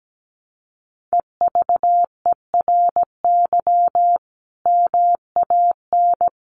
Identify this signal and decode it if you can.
Morse code, heard as a keyed tone, about 17 wpm, 700 Hz. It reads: EVERY MAN
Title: Sentences from Top 1500 Words 17wpm